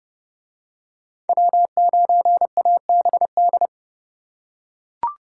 Morse code, heard as a keyed tone, about 29 wpm, 700 Hz. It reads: W9ABD